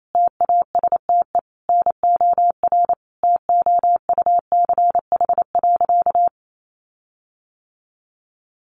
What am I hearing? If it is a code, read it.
Morse code, heard as a keyed tone, about 28 wpm, 700 Hz. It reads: TASTE NOR TOUCH.
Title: Sentences from Top 1500 Words 28wpm